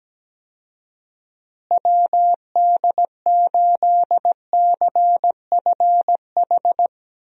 Morse code, heard as a keyed tone, about 17 wpm, 700 Hz. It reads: WD8CFH
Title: USA Call Signs - 6 Characters - Repeated 2x - 17wpm